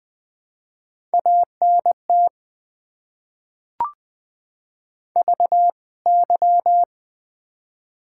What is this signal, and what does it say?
Morse code, heard as a keyed tone, about 20 wpm, 700 Hz. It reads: ANT VY
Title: Top 90 QSO Elements - 2x Character Spacing - 20wpm